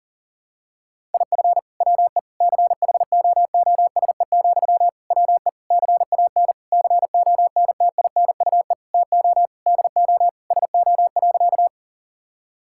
Morse code, heard as a keyed tone, about 40 wpm, 700 Hz. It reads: IF WE CHOOSE, WE CAN CONTINUE TO DO SO.
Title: Sentences from Top 1000 Words - Review Part 6 - 40wpm